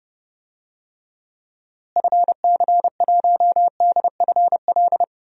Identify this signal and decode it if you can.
Morse code, heard as a keyed tone, about 32 wpm, 700 Hz. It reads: FC1DFL
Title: DXCC-100 Call Signs - 6 Characters - Repeated 2x - 30wpm